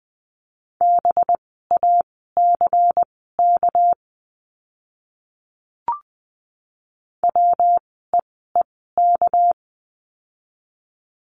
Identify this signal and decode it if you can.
Morse code, heard as a keyed tone, about 20 wpm, spaced as for 15 wpm, 700 Hz. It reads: BACK WEEK
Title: 237 Lesson 38f - Letter q - Words 20wpm